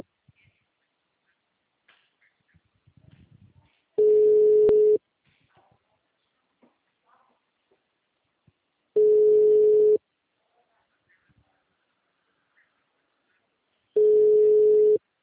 Ringback tone (Europe, 425 Hz).
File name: out- -2001- - - .83857